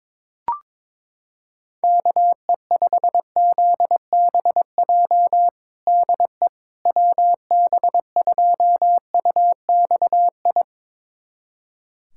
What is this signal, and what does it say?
Morse code, heard as a keyed tone, about 22 wpm, 700 Hz. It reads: KE5ZBJ DE WB2UXI
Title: USA Callsign DE Callsign 22wpm